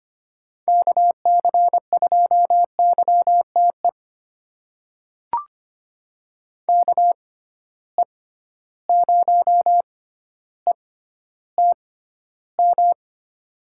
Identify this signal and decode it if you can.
Morse code, heard as a keyed tone, about 25 wpm, spaced as for 13 wpm, 700 Hz. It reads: KC2YTE KE0ETM
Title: USA Call Signs - 6 Characters - 6x Character Spacing - 25wpm